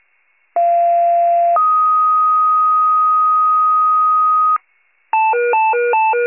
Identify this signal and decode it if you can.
Two-tone page, heard as a one-second tone, then a three-second tone, about 690 then 1250 Hz. It is heard from 0.6 to 4.6 s.